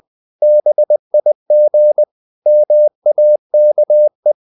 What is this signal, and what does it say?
Morse code, heard as a keyed tone, about 20 wpm, 600 Hz. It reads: BIG MAKE
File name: the old man and the sea part 2 20wpm